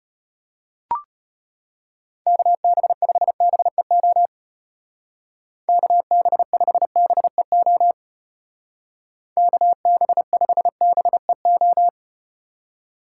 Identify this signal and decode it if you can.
Morse code, heard as a keyed tone, about 32 wpm, 700 Hz. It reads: KB5BEO KB5BEO KB5BEO